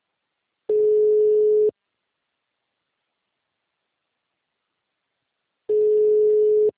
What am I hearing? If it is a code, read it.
Ringback tone (Europe, 425 Hz).
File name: out- -2002- - - .27494